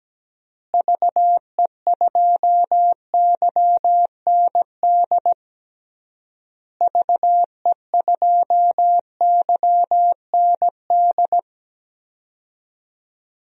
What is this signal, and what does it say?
Morse code, heard as a keyed tone, about 17 wpm, 700 Hz. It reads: VE2YND VE2YND
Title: DXCC-100 Call Signs - 6 Characters - Repeated 3x - 17wpm